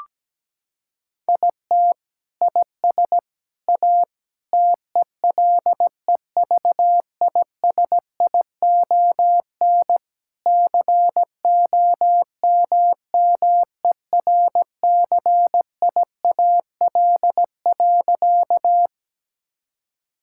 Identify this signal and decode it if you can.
Morse code, heard as a keyed tone, about 17 wpm, 700 Hz. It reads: IT IS A TELEVISION COMMERCIAL.
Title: Sentences from Top 1000 Words - Review Part 5 - 17wpm